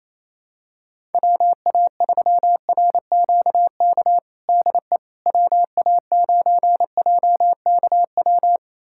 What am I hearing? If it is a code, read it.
Morse code, heard as a keyed tone, about 28 wpm, 700 Hz. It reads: WA3RQK DE WA9JKW
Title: USA Callsign DE Callsign - 6 Characters - 28wpm